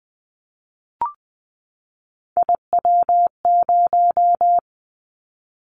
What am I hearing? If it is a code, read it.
Morse code, heard as a keyed tone, about 20 wpm, 700 Hz. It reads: IW0
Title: Callsign Prefixes for Top 25 DXCC - 20wpm